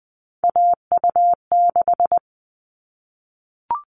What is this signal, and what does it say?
Morse code, heard as a keyed tone, about 20 wpm, 700 Hz. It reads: AU6
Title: All Cut Numbers - 3-Digit 20wpm